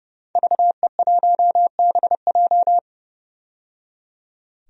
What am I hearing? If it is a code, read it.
Morse code, heard as a keyed tone, about 30 wpm, 700 Hz. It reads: VE1BJ
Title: DXCC-100 Call Signs - 5 Characters - 30wpm